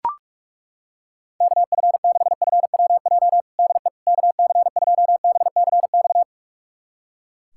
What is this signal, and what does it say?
Morse code, heard as a keyed tone, about 41 wpm, 700 Hz. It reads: KF6FWJ DE KC2BCX
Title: USA Callsign DE Callsign - 6 Characters - 45wpm